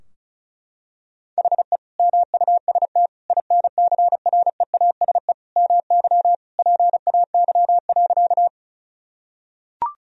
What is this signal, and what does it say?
Morse code, heard as a keyed tone, about 36 wpm, 700 Hz. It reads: HE MUST INCREASE MY PAY.
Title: Sentences from Top 1000 Words - Review Part 4 - 35wpm